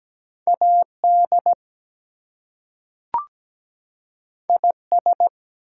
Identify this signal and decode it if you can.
Morse code, heard as a keyed tone, about 17 wpm, 700 Hz. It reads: AD IS
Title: Two Letter Words - 17wpm